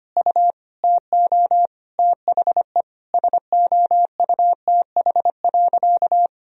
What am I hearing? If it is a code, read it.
Morse code, heard as a keyed tone, about 25 wpm, 700 Hz. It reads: U TO THE SOUTH.
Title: Sentences from Top 400 Words and Abbreviations 25wpm